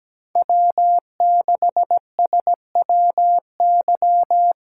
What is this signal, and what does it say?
Morse code, heard as a keyed tone, about 17 wpm, 700 Hz. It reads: W6SWY
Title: USA Callsign DE Callsign 17wpm